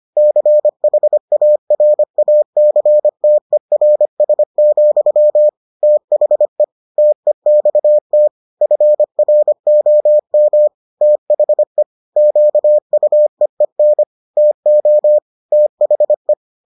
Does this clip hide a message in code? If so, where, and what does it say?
Yes: Morse code from 0.2 to 16.3 s, heard as a keyed tone, about 25 wpm, 600 Hz. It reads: CHARACTERS, THE TEXT FROM THE QUEEN TO THE